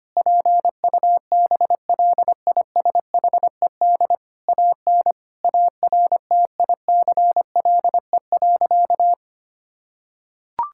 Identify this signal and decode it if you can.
Morse code, heard as a keyed tone, about 25 wpm, 700 Hz. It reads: PUBLISHED AN ARTICLE.
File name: practice / 25wpm / Sentences with 4-Word Phrases - 25wpm